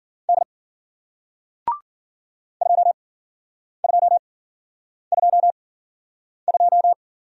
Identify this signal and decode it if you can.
Morse code, heard as a keyed tone, about 53 wpm, spaced as for 9 wpm, 700 Hz. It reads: D 2222